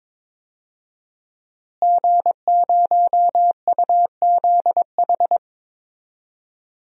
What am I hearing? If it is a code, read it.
Morse code, heard as a keyed tone, about 22 wpm, 700 Hz. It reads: G0UZH
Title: DXCC-100 Call Signs - 5 Characters - Repeated 3x - 22wpm